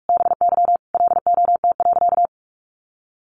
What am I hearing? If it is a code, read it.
Morse code, heard as a keyed tone, about 44 wpm, 700 Hz. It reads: BY LOT.
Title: Sentences from Top 1500 Words 45wpm